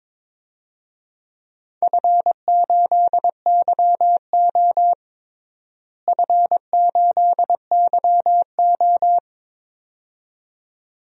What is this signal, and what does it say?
Morse code, heard as a keyed tone, about 22 wpm, 700 Hz. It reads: F8YO F8YO
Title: DXCC-100 Call Signs - 4 Characters - Repeated 2x - 22wpm